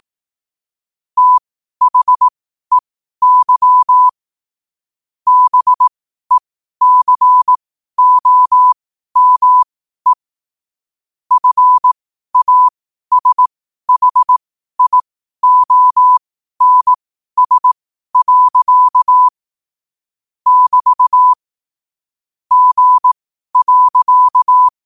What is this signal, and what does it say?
Morse code, heard as a keyed tone, about 18 wpm, spaced as for 12 wpm, 1000 Hz. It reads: THEY BECOME FASHIONS. <BT> G.
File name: 12wpm